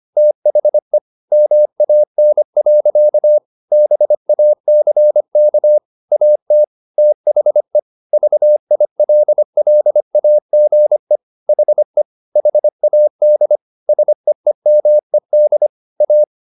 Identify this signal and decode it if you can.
Morse code, heard as a keyed tone, about 25 wpm, 600 Hz. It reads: THE MAN. BACK AT THE VILLAGE HE HAD SEEMED A